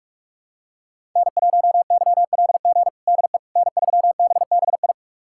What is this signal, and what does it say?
Morse code, heard as a keyed tone, about 46 wpm, 700 Hz. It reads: N1YLG DE N3BBI
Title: USA Callsign DE Callsign - 5 Characters - 45wpm